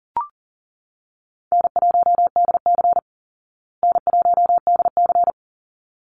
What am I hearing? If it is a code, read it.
Morse code, heard as a keyed tone, about 40 wpm, 700 Hz. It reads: N1DC N1DC
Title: Easy Call Signs Repeated 2x - 40wpm